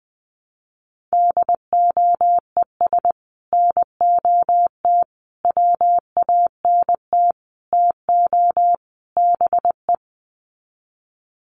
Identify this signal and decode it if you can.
Morse code, heard as a keyed tone, about 20 wpm, 700 Hz. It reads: DOES NOT WANT TO BE
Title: Sets of 5 Words Encore 20wpm